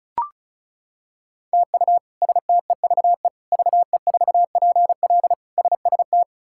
Morse code, heard as a keyed tone, about 36 wpm, 700 Hz. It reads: TU STEVE VE4PL SST
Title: K1USN SST QSO Practice - 35wpm